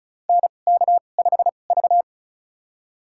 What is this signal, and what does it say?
Morse code, heard as a keyed tone, about 36 wpm, spaced as for 26 wpm, 690 Hz. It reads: NK5V
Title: USA Call Signs - 4 Characters - 2x Character Spacing - 35wpm